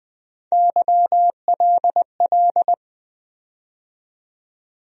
Morse code, heard as a keyed tone, about 20 wpm, 700 Hz. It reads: YLL
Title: USA Call Sign Postfixes - 3 Letters 20wpm